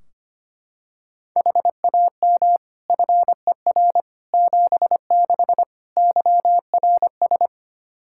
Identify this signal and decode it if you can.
Morse code, heard as a keyed tone, about 25 wpm, 700 Hz. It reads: HAM FER 76 YRS
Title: Practice QSO - Exchange 3 - More Info - 25wpm